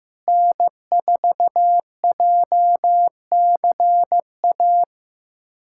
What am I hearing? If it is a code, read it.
Morse code, heard as a keyed tone, about 15 wpm, 700 Hz. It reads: N4JCA
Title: USA Callsign DE Callsign - 5 Characters - 15wpm